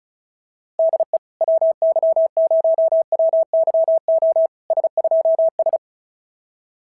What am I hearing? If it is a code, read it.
Morse code, heard as a keyed tone, about 34 wpm, 650 Hz. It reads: DE WY0WYO S2S
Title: SOTA QSO Practice 35wpm